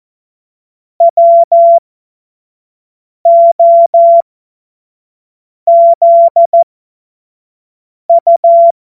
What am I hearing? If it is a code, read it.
Morse code, heard as a keyed tone, about 14 wpm, spaced as for 5 wpm, 680 Hz. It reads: WOZU
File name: Morsekurs / Witze / Witz 25. Januar 2022, 25 F70